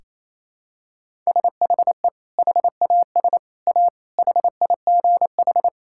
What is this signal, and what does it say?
Morse code, heard as a keyed tone, about 27 wpm, 700 Hz. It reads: SHE HAS A HIGH
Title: Sentences from Top 1000 Words - Review Part 6 - 28wpm